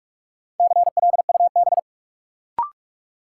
Morse code, heard as a keyed tone, about 46 wpm, 700 Hz. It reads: CLUB
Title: Sentences from Top 1200 Words - New Word Drill - 45wpm